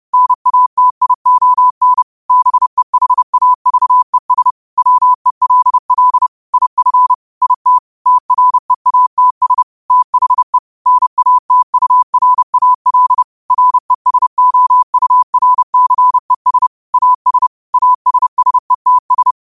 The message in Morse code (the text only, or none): NATION BEHAVES WELL IF IT TREATS THE NATURAL RESOURCES AS ASSETS